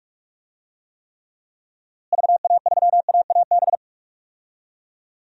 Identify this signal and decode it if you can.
Morse code, heard as a keyed tone, about 46 wpm, 700 Hz. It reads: VA3AAB
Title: DXCC-100 Call Signs - 6 Characters - Repeated 3x - 45wpm